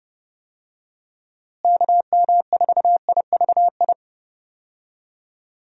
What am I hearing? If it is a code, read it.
Morse code, heard as a keyed tone, about 32 wpm, 700 Hz. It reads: KM4IVI
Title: USA Call Signs - 6 Characters - Repeated 2x - 30wpm